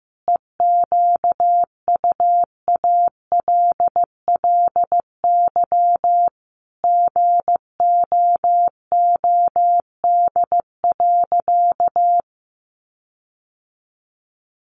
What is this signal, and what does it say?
Morse code, heard as a keyed tone, about 15 wpm, 700 Hz. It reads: EQUALLY GOOD.
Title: Sentences from Top 1400 Words 15wpm